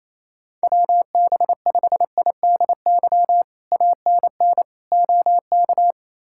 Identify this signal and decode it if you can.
Morse code, heard as a keyed tone, about 29 wpm, 700 Hz. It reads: WB5IDY ANN OK
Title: K1USN SST QSO Practice - 28wpm